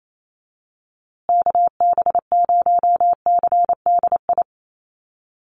Morse code, heard as a keyed tone, about 28 wpm, 700 Hz. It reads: KB0CDI
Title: USA Call Signs Repeated 2x 28wpm